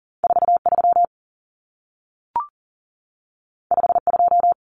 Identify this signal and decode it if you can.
Morse code, heard as a keyed tone, about 40 wpm, 700 Hz. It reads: 43 52
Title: Numbers - 2-Digit 40wpm